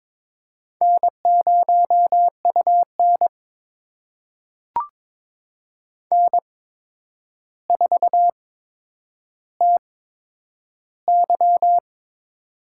Morse code, heard as a keyed tone, about 22 wpm, spaced as for 9 wpm, 700 Hz. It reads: N0UN N4TY
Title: USA Call Signs - 4 Characters - 8x Character Spacing - 22wpm